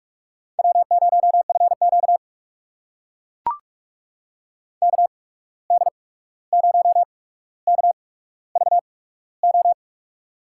Morse code, heard as a keyed tone, about 44 wpm, spaced as for 17 wpm, 700 Hz. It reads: W0FQ KD0KVO
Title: USA Call Signs - 8x Character Spacing - 45wpm